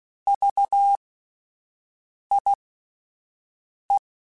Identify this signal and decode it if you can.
Morse code, heard as a keyed tone, about 16 wpm, spaced as for 5 wpm, 800 Hz. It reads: VIE